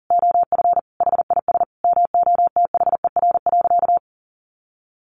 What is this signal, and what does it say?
Morse code, heard as a keyed tone, about 40 wpm, 700 Hz. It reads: OF HIS MOTHER.
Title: Sentences from Top 1300 Words 40wpm